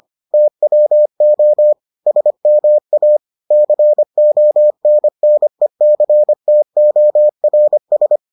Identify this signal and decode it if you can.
Morse code, heard as a keyed tone, about 25 wpm, 600 Hz. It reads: TWO SMA CONNECTORS